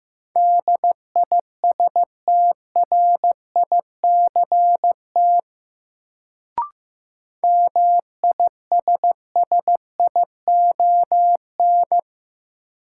Morse code, heard as a keyed tone, about 15 wpm, 700 Hz. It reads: DISTRICT MISSION
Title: Sentences from Top 1600 Words - New Word Drill Alt - 15wpm